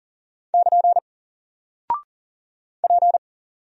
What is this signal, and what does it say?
Morse code, heard as a keyed tone, about 40 wpm, 700 Hz. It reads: <KN> P
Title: Single Character (Letter:Number:Punctuation:Prosign) 40wpm